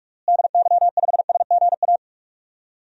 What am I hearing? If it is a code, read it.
Morse code, heard as a keyed tone, about 46 wpm, 690 Hz. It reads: DY5SGA
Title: DXCC-100 Call Signs - 6 Characters - Repeated 2x - 45wpm